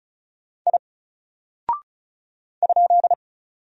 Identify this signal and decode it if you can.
Morse code, heard as a keyed tone, about 36 wpm, 700 Hz. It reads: I ?
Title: Single Character (Letter:Number:Punctuation:Prosign) 35wpm